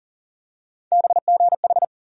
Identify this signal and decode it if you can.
Morse code, heard as a keyed tone, about 40 wpm, 700 Hz. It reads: BGH
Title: USA Call Sign Postfixes - 3 Letters 40wpm